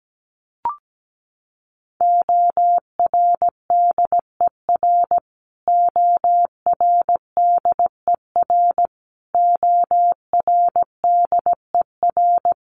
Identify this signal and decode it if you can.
Morse code, heard as a keyed tone, about 17 wpm, 700 Hz. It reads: ORDER ORDER ORDER
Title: Top 500 Words Repeated 17wpm